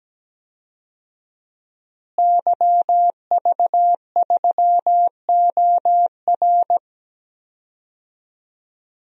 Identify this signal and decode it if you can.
Morse code, heard as a keyed tone, about 17 wpm, 700 Hz. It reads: YV3OR
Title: DXCC-100 Call Signs - 5 Characters - Repeated 3x - 17wpm